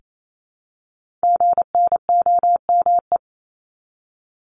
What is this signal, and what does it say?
Morse code, heard as a keyed tone, about 28 wpm, 700 Hz. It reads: GNOME
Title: Multi-letter Phonemes Repeated 3x 28wpm